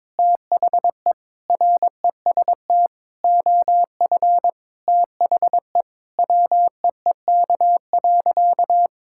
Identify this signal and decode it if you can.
Morse code, heard as a keyed tone, about 21 wpm, 700 Hz. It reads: THE REST OF THE WEEK.
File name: practice / 22wpm / Sentences with 4-Word Phrases - 22wpm